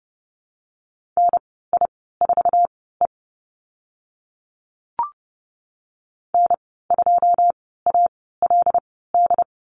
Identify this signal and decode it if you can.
Morse code, heard as a keyed tone, about 30 wpm, spaced as for 17 wpm, 700 Hz. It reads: NI4E N2ALD
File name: Lesson 14g - Number 2 - Call Signs 30wpm